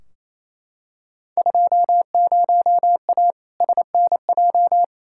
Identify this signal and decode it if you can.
Morse code, heard as a keyed tone, about 28 wpm, 700 Hz. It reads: 20A SNJ
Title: Field Day Exchanges Encore - 28wpm